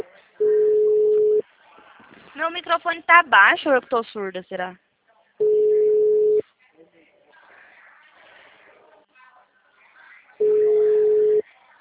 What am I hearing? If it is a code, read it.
Ringback tone (Europe, 425 Hz), with speech.